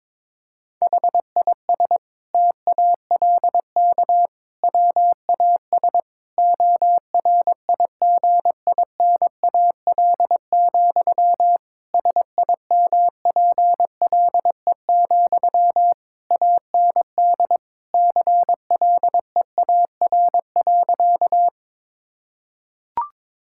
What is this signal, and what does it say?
Morse code, heard as a keyed tone, about 22 wpm, 700 Hz. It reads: HIS TALK WAS ORIGINAL, SIMPLE, AND CLEAR.
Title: Sentences from Top 1000 Words - Review Part 6 - 22wpm